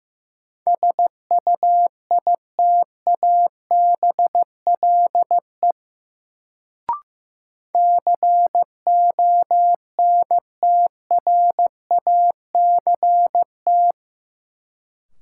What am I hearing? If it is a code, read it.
Morse code, heard as a keyed tone, about 15 wpm, 700 Hz. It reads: SUITABLE CONTRACT
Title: Sentences from Top 1400 Words - New Word Drill - 15wpm